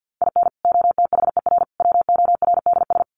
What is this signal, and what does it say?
Morse code, heard as a keyed tone, about 50 wpm, 700 Hz. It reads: IN OTHER WORDS
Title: Sets of 3 Words 50wpm